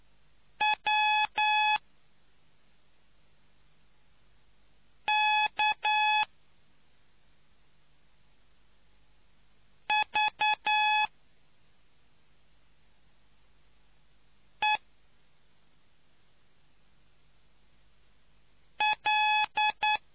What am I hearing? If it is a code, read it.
Morse code, heard as a keyed tone, about 9 wpm, spaced as for 2 wpm, 850 Hz. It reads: WKVEL